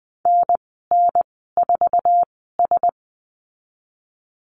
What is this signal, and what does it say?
Morse code, heard as a keyed tone, about 20 wpm, spaced as for 14 wpm, 700 Hz. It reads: NN4S
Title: 082 Lesson 14g - Number 2 - Call Signs 20wpm